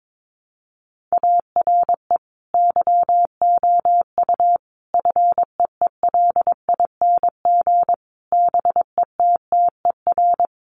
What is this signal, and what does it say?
Morse code, heard as a keyed tone, about 22 wpm, 700 Hz. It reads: ARE YOU FEELING BETTER
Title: Sentences from Top 1200 Words 22wpm